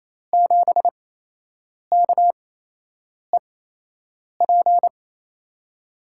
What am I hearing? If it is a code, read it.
Morse code, heard as a keyed tone, about 28 wpm, spaced as for 8 wpm, 700 Hz. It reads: 7KEP
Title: USA Call Signs - 6 Characters - 8x Character Spacing - 28wpm